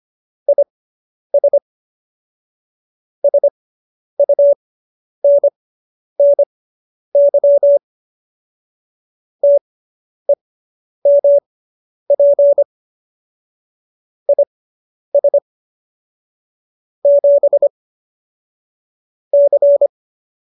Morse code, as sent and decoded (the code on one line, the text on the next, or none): .. ... / ... ..- -. -. -.-- / - . -- .--. / .. ... / --... / -.-.
IS SUNNY TEMP IS 7 C